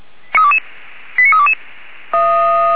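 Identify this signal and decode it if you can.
Five-tone selective call (ZVEI), heard as quick runs of five tones, tones of about 80 ms. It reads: 89322
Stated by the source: ZVEI five-tone selective call, code 89326, received by SDR.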